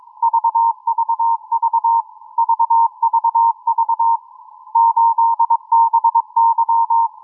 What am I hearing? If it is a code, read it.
Morse code, heard as a keyed tone, about 22 wpm, 920 Hz. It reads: VVV VVV 8BY